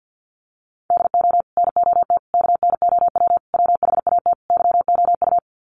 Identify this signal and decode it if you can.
Morse code, heard as a keyed tone, about 50 wpm, 700 Hz. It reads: DO NOT KNOW WHAT YOU
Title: Sets of 5 Words 50wpm